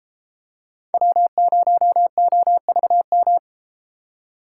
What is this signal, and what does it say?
Morse code, heard as a keyed tone, about 33 wpm, 700 Hz. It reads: W0OVM